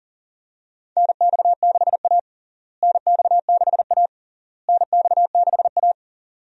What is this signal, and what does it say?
Morse code, heard as a keyed tone, about 40 wpm, 700 Hz. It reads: NX6A NX6A NX6A
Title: USA Call Signs - 4 Characters - Repeated 3x - 40wpm